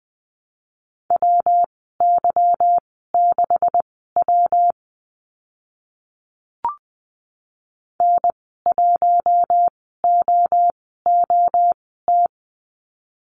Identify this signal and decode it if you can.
Morse code, heard as a keyed tone, about 20 wpm, spaced as for 15 wpm, 700 Hz. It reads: WY6W N1OOT
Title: 161 Lesson 26g - Letter g - Call Signs 20wpm